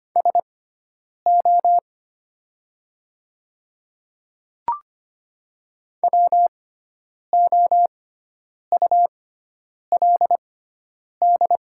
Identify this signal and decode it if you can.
Morse code, heard as a keyed tone, about 25 wpm, spaced as for 9 wpm, 700 Hz. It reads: SO WOULD
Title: 6x Character Spacing - Top 100 Words - 25wpm